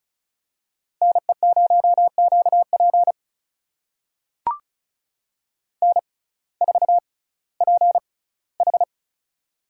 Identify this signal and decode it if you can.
Morse code, heard as a keyed tone, about 36 wpm, spaced as for 19 wpm, 700 Hz. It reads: NE0QP N4PH
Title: USA Call Signs - 6x Character Spacing - 35wpm